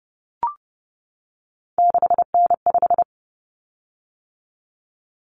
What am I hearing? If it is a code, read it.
Morse code, heard as a keyed tone, about 30 wpm, 700 Hz. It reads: 6N5
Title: Common Cut Numbers - 3-Digit 30wpm